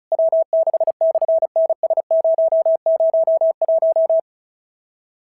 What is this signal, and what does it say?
Morse code, heard as a keyed tone, about 36 wpm, 690 Hz. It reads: W6/NS001
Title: SOTA QSO Practice 35wpm